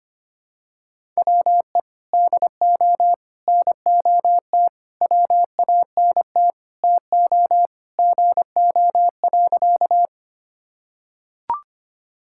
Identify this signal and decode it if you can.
Morse code, heard as a keyed tone, about 25 wpm, 700 Hz. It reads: WE DO NOT WANT TO GO.
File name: practice / 25wpm / Sentences with 4-Word Phrases - 25wpm